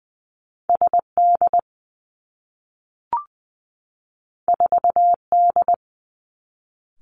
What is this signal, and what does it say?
Morse code, heard as a keyed tone, about 20 wpm, 700 Hz. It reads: S D 4 D
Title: Two Challenging Letter-Number Combinations 20wpm